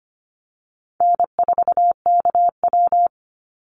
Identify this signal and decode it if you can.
Morse code, heard as a keyed tone, about 25 wpm, 700 Hz. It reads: N4KW
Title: Easy Call Signs Repeated 2x - 25wpm